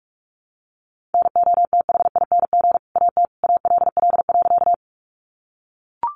Morse code, heard as a keyed tone, about 44 wpm, 700 Hz. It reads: NOTHING AT ALL.
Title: Sentences from Top 500 Words 45wpm